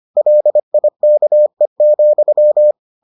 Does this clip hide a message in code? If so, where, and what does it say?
Yes: Morse code from 0.2 to 2.7 s, heard as a keyed tone, about 25 wpm, 600 Hz. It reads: LIKE,